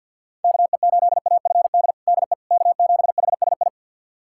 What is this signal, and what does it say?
Morse code, heard as a keyed tone, about 51 wpm, 700 Hz. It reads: KE8AFD DE K7HSI
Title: USA Callsign DE Callsign 50wpm